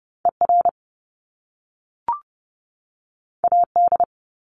Morse code, heard as a keyed tone, about 30 wpm, 700 Hz. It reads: ER AD
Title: Top Two Letter Combinations in Words 30wpm